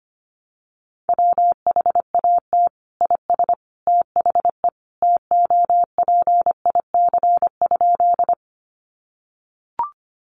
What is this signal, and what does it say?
Morse code, heard as a keyed tone, about 25 wpm, 700 Hz. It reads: WHAT IS THE TOPIC?
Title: Sentences from Top 1200 Words 25wpm